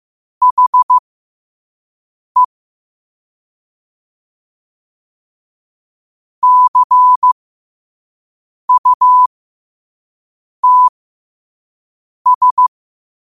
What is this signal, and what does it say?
Morse code, heard as a keyed tone, about 15 wpm, spaced as for 5 wpm, 1000 Hz. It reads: HE CUTS
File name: qotdfiles / 5wpm